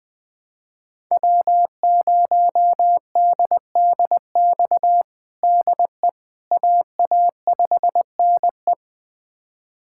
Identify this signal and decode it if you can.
Morse code, heard as a keyed tone, about 20 wpm, 700 Hz. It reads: W0DDX DE AA5NE
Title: USA Callsign DE Callsign - 5 Characters - 20wpm